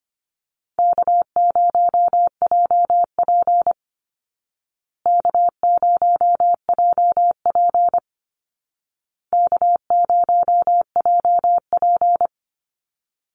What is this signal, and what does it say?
Morse code, heard as a keyed tone, about 24 wpm, 700 Hz. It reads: K0JP K0JP K0JP
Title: Easy Call Signs Repeated 3x - 25wpm